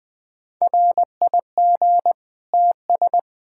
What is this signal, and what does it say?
Morse code, heard as a keyed tone, about 21 wpm, 700 Hz. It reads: RIG TS
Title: K1USN SST QSO Practice 20wpm